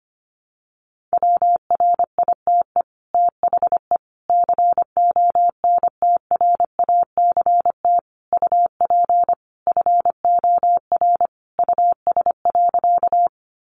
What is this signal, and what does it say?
Morse code, heard as a keyed tone, about 25 wpm, 700 Hz. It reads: WRITE THE CONTRACT UP FOR US.
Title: Sentences from Top 1400 Words 25wpm